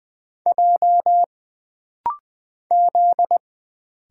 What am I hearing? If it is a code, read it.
Morse code, heard as a keyed tone, about 20 wpm, 700 Hz. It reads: J Z
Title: Single Letters - Rapid-Fire 20wpm